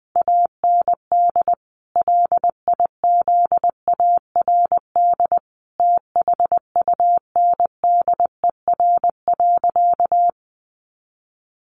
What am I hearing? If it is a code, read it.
Morse code, heard as a keyed tone, about 20 wpm, 700 Hz. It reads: AND LIZARD THUNDER.
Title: A Sound Of Thunder 20wpm